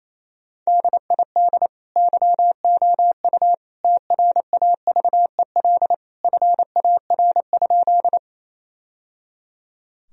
Morse code, heard as a keyed tone, about 28 wpm, 700 Hz. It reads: DID YOU TRAVEL FAR?
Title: Sentences from Top 1000 Words - Review Part 2 - 28wpm